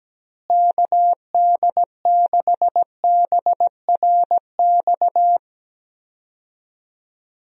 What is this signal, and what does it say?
Morse code, heard as a keyed tone, about 17 wpm, 700 Hz. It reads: KD6BRX
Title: USA Call Signs - 6 Characters - Repeated 3x - 17wpm